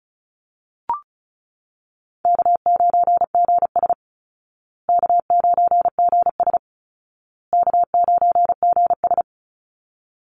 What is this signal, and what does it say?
Morse code, heard as a keyed tone, about 34 wpm, 700 Hz. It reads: K9GS K9GS K9GS
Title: Easy Call Signs Repeated 3x - 35wpm